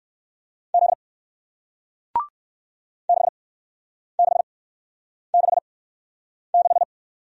Morse code, heard as a keyed tone, about 57 wpm, spaced as for 9 wpm, 700 Hz. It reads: Z 6666